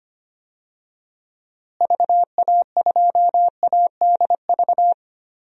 Morse code, heard as a keyed tone, about 25 wpm, 700 Hz. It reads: VA2ADV
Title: DXCC-100 Call Signs - 6 Characters - Repeated 3x - 25wpm